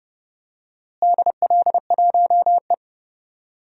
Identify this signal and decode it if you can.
Morse code, heard as a keyed tone, about 30 wpm, 700 Hz. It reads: DL1E
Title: DXCC-100 Call Signs - 4 Characters - Repeated 2x - 30wpm